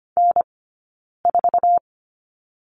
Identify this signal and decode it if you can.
Morse code, heard as a keyed tone, about 25 wpm, spaced as for 9 wpm, 700 Hz. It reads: N4
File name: Lesson 15d - Number 5 - Two Characters 25wpm